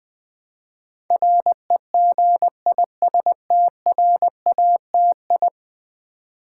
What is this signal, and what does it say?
Morse code, heard as a keyed tone, about 20 wpm, 700 Hz. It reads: REGISTRATI
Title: Build A Word - 5 Letters At A Time - 12-Letter Words - 20wpm